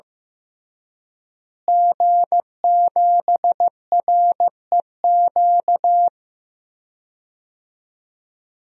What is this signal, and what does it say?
Morse code, heard as a keyed tone, about 16 wpm, 700 Hz. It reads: G7REQ
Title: DXCC-100 Call Signs - 5 Characters - Repeated 3x - 15wpm